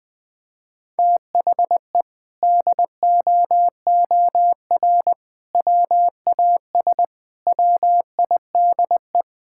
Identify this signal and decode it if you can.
Morse code, heard as a keyed tone, about 20 wpm, 700 Hz. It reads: THE DOOR WAS WIDE
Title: Sentences from Top 1000 Words - Review Part 6 - 20wpm